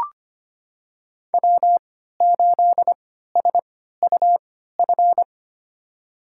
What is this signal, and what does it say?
Morse code, heard as a keyed tone, about 25 wpm, spaced as for 14 wpm, 700 Hz. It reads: W8SUF
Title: USA Call Signs - 3x Character Spacing - 25wpm